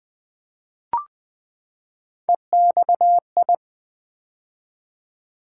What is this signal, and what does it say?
Morse code, heard as a keyed tone, about 20 wpm, 700 Hz. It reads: EXI
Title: Build A Word - 3 Letters At A Time - 9-Letter Words - 20wpm